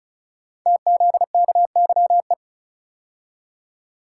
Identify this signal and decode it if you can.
Morse code, heard as a keyed tone, about 36 wpm, 690 Hz. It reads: TZKYE
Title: Code Groups - 5 Letters - 35wpm